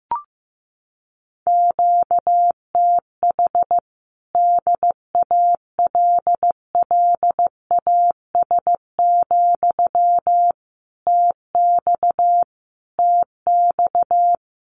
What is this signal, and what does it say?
Morse code, heard as a keyed tone, about 15 wpm, 700 Hz. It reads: QTH DALLAS, TX TX
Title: Common QSO Phrases Encore 15wpm